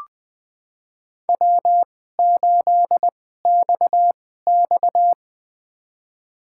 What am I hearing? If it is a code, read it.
Morse code, heard as a keyed tone, about 20 wpm, spaced as for 15 wpm, 700 Hz. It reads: W8XX
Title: USA Call Signs - 4 Characters - 2x Character Spacing - 20wpm